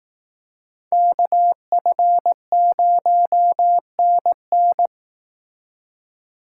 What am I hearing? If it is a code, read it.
Morse code, heard as a keyed tone, about 18 wpm, 700 Hz. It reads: KF0NN
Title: Speed Racing - USA 2x2 Call Signs 15wpm